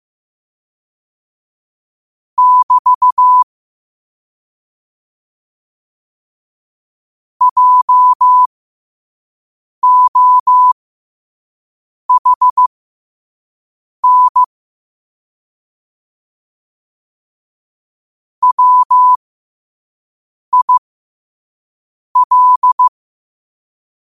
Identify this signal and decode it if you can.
Morse code, heard as a keyed tone, about 15 wpm, spaced as for 5 wpm, 1000 Hz. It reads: <BT> JOHN WIL
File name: qotdfiles / 5wpm